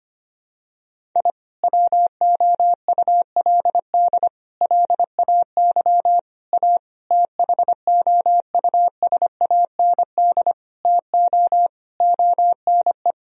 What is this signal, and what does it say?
Morse code, heard as a keyed tone, about 25 wpm, 700 Hz. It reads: I WOULD LAY A THOUSAND TO ONE
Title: Sentences from Top 1000 Words - Review Part 3 - 25wpm